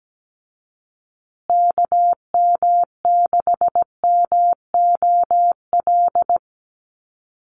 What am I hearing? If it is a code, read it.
Morse code, heard as a keyed tone, about 17 wpm, 700 Hz. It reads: KM6MOL
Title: USA Call Signs Repeated 2x 17wpm